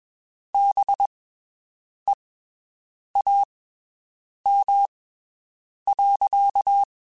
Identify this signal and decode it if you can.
Morse code, heard as a keyed tone, about 22 wpm, spaced as for 7 wpm, 800 Hz. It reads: BEAM.
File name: (spaced) / QSOs Spaced-mp3s / WD0BGP 1 @20 sp